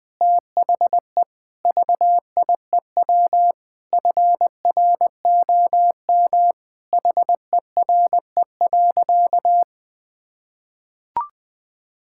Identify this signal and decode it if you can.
Morse code, heard as a keyed tone, about 20 wpm, 700 Hz. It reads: THE VIEW FROM HERE.
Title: Sentences from Top 1000 Words - Review Part 6 - 20wpm